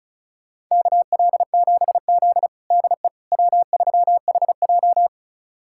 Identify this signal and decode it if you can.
Morse code, heard as a keyed tone, about 36 wpm, 700 Hz. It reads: KL7Z DE W3HJ
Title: USA Callsign DE Callsign - 4 Characters - 35wpm